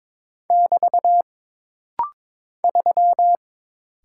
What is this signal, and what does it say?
Morse code, heard as a keyed tone, about 22 wpm, 700 Hz. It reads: <BT> 3
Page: This rapid-fire timing is geared to push your ICR (Instant Character Recognition) skills.